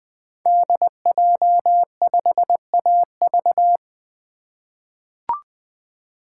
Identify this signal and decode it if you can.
Morse code, heard as a keyed tone, about 20 wpm, 700 Hz. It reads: DJ5AV
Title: 2023 SOTA Chaser Call Signs - 20wpm